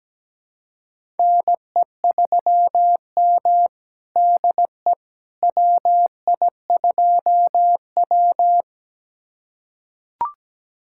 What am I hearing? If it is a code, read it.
Morse code, heard as a keyed tone, about 17 wpm, 700 Hz. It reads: NE3M DE WI2W
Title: USA Callsign DE Callsign - 4 Characters - 17wpm